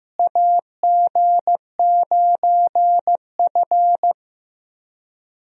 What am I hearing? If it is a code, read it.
Morse code, heard as a keyed tone, about 15 wpm, 700 Hz. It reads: AG9F
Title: USA Call Signs - 4 Characters - Repeated 3x - 15wpm